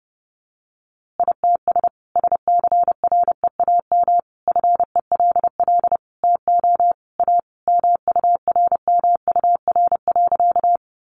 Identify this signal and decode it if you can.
Morse code, heard as a keyed tone, about 30 wpm, 700 Hz. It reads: ITS SCREAM FELL TO A MURMUR.
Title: A Sound Of Thunder 30wpm